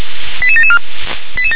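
Five-tone selective call (ZVEI), heard as quick runs of five tones, tones of about 70 ms. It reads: 88774
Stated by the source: Received on SDR as ZVEI five-tone selective call.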